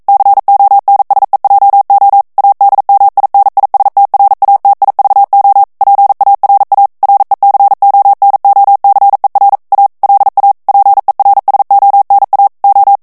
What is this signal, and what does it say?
Morse code, heard as a keyed tone, about 42 wpm, 800 Hz. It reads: CONSEJO ADMINISTRATIVO PARA RECONOCER A LA PERSONA O
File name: morse mp3 / ESPAGNOL TEXTE EN CLAIR 1 VITESSE 40 MOTS MINUTES